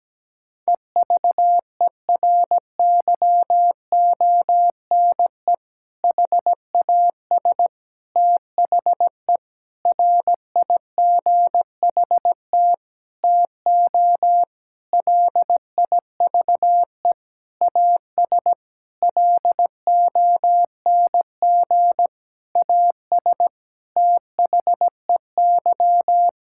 Morse code, heard as a keyed tone, about 17 wpm, 700 Hz. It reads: EVERYONE HAS THE RIGHT TO LIVE AS LONG AS THEY
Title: Sentences from Top 1000 Words - Review Part 5 - 17wpm